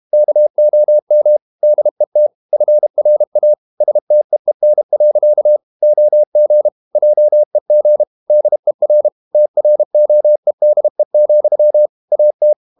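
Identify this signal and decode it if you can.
Morse code, heard as a keyed tone, about 32 wpm, 600 Hz. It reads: KOM DET FRA STEEN. OG JEG DER TROEDE, AT
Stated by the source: High-speed QRQ CW at 32 WPM.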